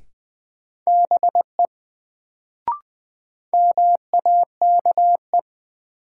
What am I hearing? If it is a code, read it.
Morse code, heard as a keyed tone, about 20 wpm, 700 Hz. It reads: BE MAKE
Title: Sets of 2 Words - New Word Drill - 20wpm